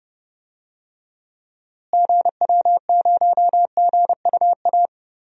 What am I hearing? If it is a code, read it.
Morse code, heard as a keyed tone, about 30 wpm, 700 Hz. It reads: GW0GUA
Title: DXCC-100 Call Signs - 6 Characters - Repeated 2x - 30wpm